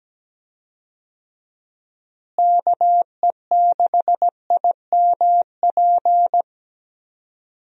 Morse code, heard as a keyed tone, about 17 wpm, 700 Hz. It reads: KE6IMP
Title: USA Call Signs - 6 Characters - Repeated 3x - 17wpm